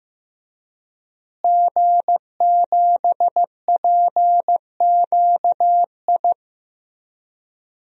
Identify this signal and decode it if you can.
Morse code, heard as a keyed tone, about 15 wpm, 700 Hz. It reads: G7PQI
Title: DXCC-100 Call Signs - 5 Characters - Repeated 2x - 15wpm